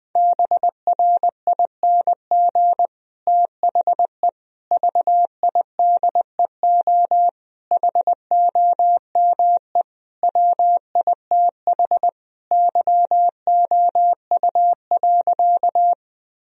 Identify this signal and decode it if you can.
Morse code, heard as a keyed tone, about 20 wpm, 700 Hz. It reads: BRING THE VIDEO HOME WITH YOU.
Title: Sentences from Top 1000 Words - Review Part 4 - 20wpm